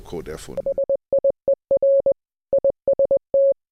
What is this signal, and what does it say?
Morse code, heard as a keyed tone, about 20 wpm, 550 Hz, with speech. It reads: SIER IST